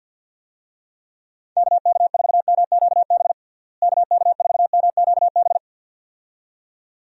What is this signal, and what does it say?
Morse code, heard as a keyed tone, about 51 wpm, 700 Hz. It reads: KK4MQB KK4MQB
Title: USA Call Signs - 6 Characters - Repeated 2x - 50wpm